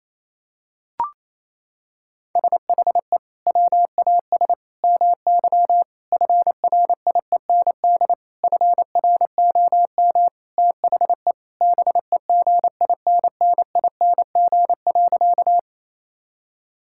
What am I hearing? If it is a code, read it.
Morse code, heard as a keyed tone, about 28 wpm, 700 Hz. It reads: SHE WAS MY FRIEND FROM THE BEGINNING.
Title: Sentences from Top 1000 Words - Review Part 5 - 28wpm